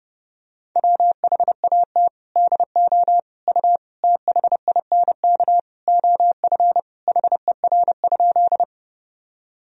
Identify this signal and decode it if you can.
Morse code, heard as a keyed tone, about 32 wpm, 700 Hz. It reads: WHAT DO U THINK OF HER?
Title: Sentences from Top 200 Words and Abbreviations 30wpm